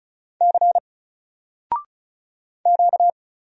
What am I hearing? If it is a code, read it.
Morse code, heard as a keyed tone, about 36 wpm, 700 Hz. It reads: C Q
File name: Single-Letter-Number-35wpm